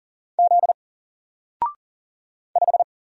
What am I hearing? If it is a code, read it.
Morse code, heard as a keyed tone, about 40 wpm, 700 Hz. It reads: Z 5
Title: Single Character (Letter:Number:Punctuation:Prosign) Alt 40wpm